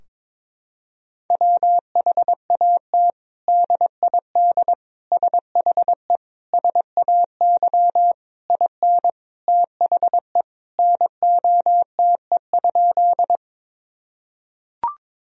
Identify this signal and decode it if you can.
Morse code, heard as a keyed tone, about 21 wpm, 700 Hz. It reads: WHAT DID SHE SAY IN THE NOTE?
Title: Sentences from Top 1000 Words - Review Part 2 - 22wpm